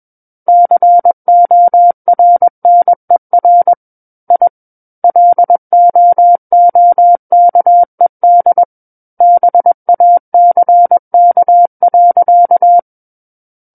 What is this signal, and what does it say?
Morse code, heard as a keyed tone, about 21 wpm, 700 Hz. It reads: CORNER I LOOKED BACK.
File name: Book One / 12-Chapter 12 - 21 WPM